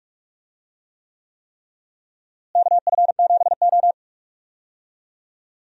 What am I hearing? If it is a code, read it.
Morse code, heard as a keyed tone, about 46 wpm, 700 Hz. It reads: KF7O
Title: USA Call Signs - 4 Characters - Repeated 3x - 45wpm